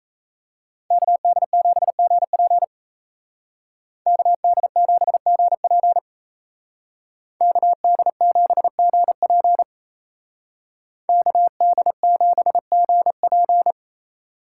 Morse code, heard as a keyed tone, about 36 wpm, 700 Hz. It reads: KD7GP KD7GP KD7GP KD7GP